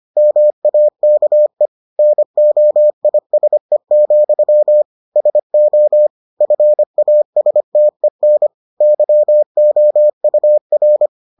Morse code, heard as a keyed tone, about 25 wpm, 600 Hz. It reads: MAKE NOISE, SO FASTEN YOUR